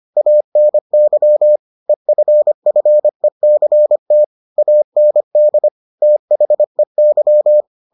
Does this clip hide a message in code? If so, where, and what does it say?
Yes: Morse code from 0.2 to 7.6 s, heard as a keyed tone, about 25 wpm, 600 Hz. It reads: ANY EFFECT AND THEY